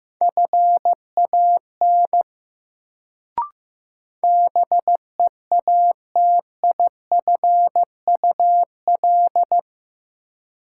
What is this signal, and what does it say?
Morse code, heard as a keyed tone, about 15 wpm, 700 Hz. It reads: FAN BEATIFUL
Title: Sentences from Top 1300 Words - New Word Drill Alt - 15wpm